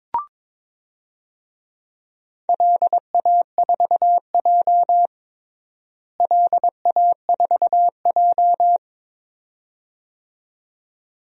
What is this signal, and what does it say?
Morse code, heard as a keyed tone, about 22 wpm, 700 Hz. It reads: LA4J LA4J
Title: DXCC-100 Call Signs - 4 Characters - Repeated 2x - 22wpm